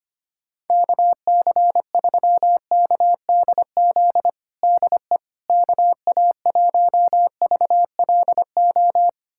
Morse code, heard as a keyed tone, about 25 wpm, 700 Hz. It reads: KC3KDZ DE KA1VLO
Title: USA Callsign DE Callsign - 6 Characters - 25wpm